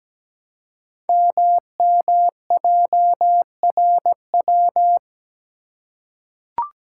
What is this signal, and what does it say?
Morse code, heard as a keyed tone, about 17 wpm, 700 Hz. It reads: MMJRW
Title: Code Groups - 5 Letters - 17wpm